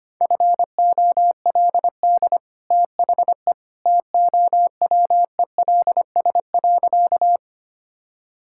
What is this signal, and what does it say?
Morse code, heard as a keyed tone, about 25 wpm, 700 Hz. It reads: FOLD THE TOWELS.
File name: Easy Sentences with Hints - 25wpm